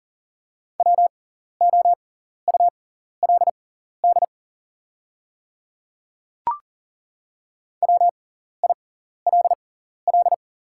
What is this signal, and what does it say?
Morse code, heard as a keyed tone, about 40 wpm, spaced as for 14 wpm, 690 Hz. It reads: WOULD WILL
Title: 6x Character Spacing - Top 100 Words - 40wpm